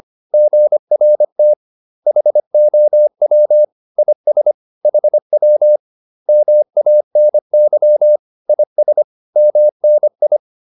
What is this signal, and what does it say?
Morse code, heard as a keyed tone, about 25 wpm, 600 Hz. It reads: GRT HOW IS HW MANY IS MNI